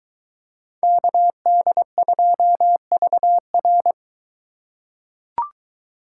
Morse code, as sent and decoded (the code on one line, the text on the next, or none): -.- -.. ..--- ...- .-.
KD2VR